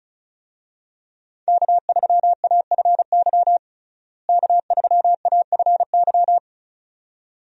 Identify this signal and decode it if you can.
Morse code, heard as a keyed tone, about 34 wpm, 700 Hz. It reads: K3AFY K3AFY
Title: USA Call Signs - 5 Characters - Repeated 2x - 35wpm